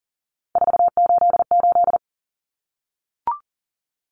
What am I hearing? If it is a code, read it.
Morse code, heard as a keyed tone, about 40 wpm, 700 Hz. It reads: 488
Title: Common Cut Numbers - 3-Digit 40wpm